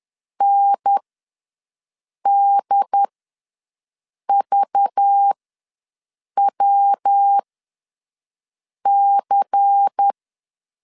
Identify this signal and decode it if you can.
Morse code, heard as a keyed tone, about 10 wpm, spaced as for 5 wpm, 800 Hz. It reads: NDVWC
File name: LEZIONI / 5 LBVC / 5 LBVC ws 10wpm 3733